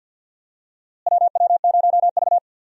Morse code, heard as a keyed tone, about 47 wpm, 700 Hz. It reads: WW0V
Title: USA Call Signs - 4 Characters - Repeated 3x - 50wpm